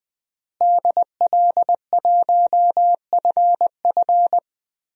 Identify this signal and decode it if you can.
Morse code, heard as a keyed tone, about 20 wpm, 700 Hz. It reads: DL1FF
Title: DXCC-100 Call Signs - 5 Characters - Repeated 2x - 20wpm